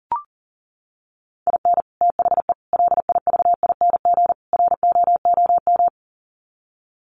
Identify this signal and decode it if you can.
Morse code, heard as a keyed tone, about 40 wpm, 700 Hz. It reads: IN THE LIVING ROOM
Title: Sets of 4 Words 40wpm